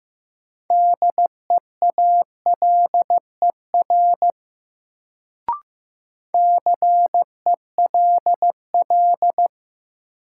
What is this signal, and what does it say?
Morse code, heard as a keyed tone, about 15 wpm, 700 Hz. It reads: DEALER CELL
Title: Sentences from Top 1100 Words - New Word Drill Alt - 15wpm